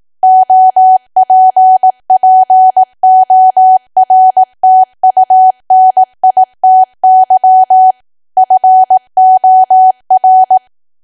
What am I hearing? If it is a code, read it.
Morse code, heard as a keyed tone, about 18 wpm, 750 Hz. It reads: OPPORTUNITY FOR
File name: cwa / W1AW-Level2 / 18WPM